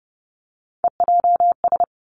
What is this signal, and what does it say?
Morse code, heard as a keyed tone, about 30 wpm, 700 Hz. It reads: EJS
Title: Three Challenging Letter-Number Combinations 30wpm